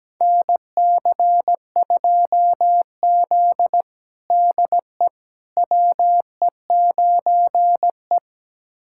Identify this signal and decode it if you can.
Morse code, heard as a keyed tone, about 17 wpm, 700 Hz. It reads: NC2Z DE WE9E
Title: USA Callsign DE Callsign - 4 Characters - 17wpm